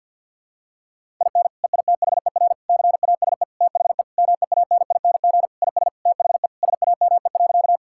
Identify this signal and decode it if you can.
Morse code, heard as a keyed tone, about 50 wpm, 700 Hz. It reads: IN EITHER CASE THE MEANING IS THE SAME.
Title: Sentences from Top 1000 Words - Review Part 6 - 50wpm